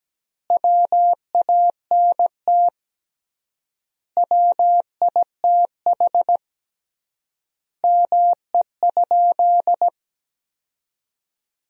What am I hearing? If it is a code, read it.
Morse code, heard as a keyed tone, about 17 wpm, 700 Hz. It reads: WANT WITH ME?
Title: 3x Word-Spacing - Sentences from Top 200 Words - 17wpm